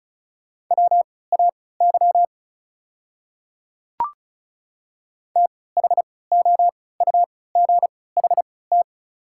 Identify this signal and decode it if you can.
Morse code, heard as a keyed tone, about 36 wpm, spaced as for 20 wpm, 690 Hz. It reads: WAY THOUGHT
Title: 3x Character Spacing - Top 300 Words - 35wpm